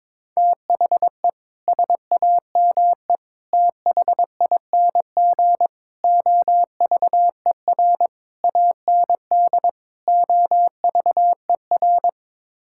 Morse code, heard as a keyed tone, about 22 wpm, 700 Hz. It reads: THE SAME THING OVER AND OVER
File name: practice / 22wpm / Sets of 6 Words - 22wpm